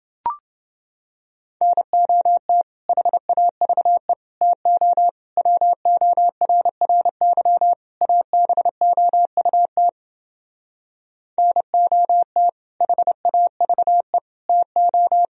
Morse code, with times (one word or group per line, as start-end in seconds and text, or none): 1.6-2.6 NOT
2.9-4.1 HAVE
4.4-5.1 TO
5.4-7.7 WORRY
8.0-9.9 ABOUT
11.4-12.5 NOT
12.8-14.2 HAVE
14.5-15.3 TO